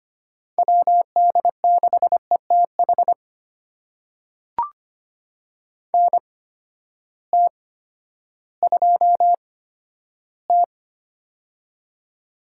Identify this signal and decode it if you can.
Morse code, heard as a keyed tone, about 25 wpm, spaced as for 12 wpm, 700 Hz. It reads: WD6ETH NT2T
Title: USA Call Signs - 8x Character Spacing - 25wpm